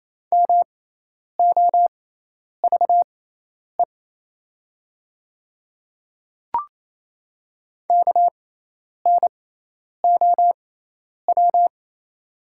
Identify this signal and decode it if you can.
Morse code, heard as a keyed tone, about 28 wpm, spaced as for 10 wpm, 700 Hz. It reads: MOVE KNOW
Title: 6x Character Spacing - Top 200 Words - 28wpm